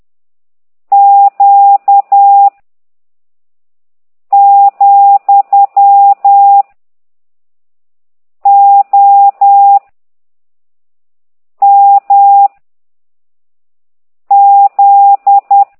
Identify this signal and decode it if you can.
Morse code, heard as a keyed tone, about 10 wpm, 800 Hz. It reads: Q , O M Z
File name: ct2idw / exerc13